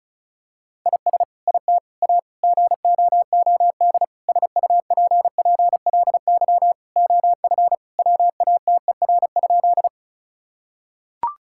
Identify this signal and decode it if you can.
Morse code, heard as a keyed tone, about 36 wpm, 700 Hz. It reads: IS IT A GOOD SUPPLY OF WATER?
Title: Sentences from Top 1000 Words - Review Part 5 - 35wpm